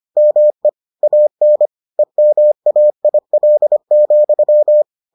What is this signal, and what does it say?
Morse code, heard as a keyed tone, about 25 wpm, 600 Hz. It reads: ME AN EMAIL,